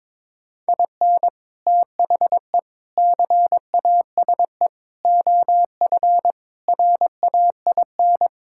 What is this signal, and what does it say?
Morse code, heard as a keyed tone, about 21 wpm, 700 Hz. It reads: IN THE CASE OF RAIN
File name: practice / 22wpm / Sentences with 3-Word Phrases - 22wpm